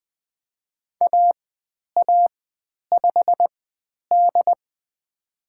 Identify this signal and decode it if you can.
Morse code, heard as a keyed tone, about 20 wpm, spaced as for 10 wpm, 700 Hz. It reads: AA5D
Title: 102 Lesson 17g - Letter u - Call Signs 20wpm